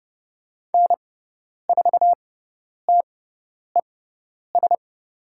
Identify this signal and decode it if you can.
Morse code, heard as a keyed tone, about 30 wpm, spaced as for 10 wpm, 700 Hz. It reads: N4TES
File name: Lesson 17g - Letter u - Call Signs 30wpm